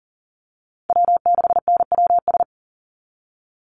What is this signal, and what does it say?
Morse code, heard as a keyed tone, about 40 wpm, 700 Hz. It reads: W6NWS
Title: Call Signs Repeated 2x - 40wpm